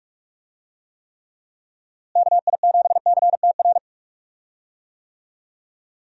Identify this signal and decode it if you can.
Morse code, heard as a keyed tone, about 44 wpm, 700 Hz. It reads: KI7CTR
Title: USA Call Signs - 6 Characters - Repeated 2x - 45wpm